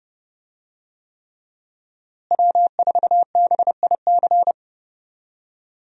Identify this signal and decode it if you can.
Morse code, heard as a keyed tone, about 31 wpm, 700 Hz. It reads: W4BIC
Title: USA Call Signs - 5 Characters - Repeated 3x - 30wpm